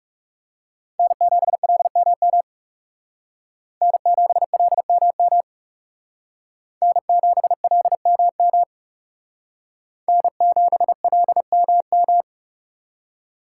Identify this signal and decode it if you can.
Morse code, heard as a keyed tone, about 37 wpm, 700 Hz. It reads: N7LMM N7LMM N7LMM N7LMM
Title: Speed Racing - USA 1x3 Call Signs 30wpm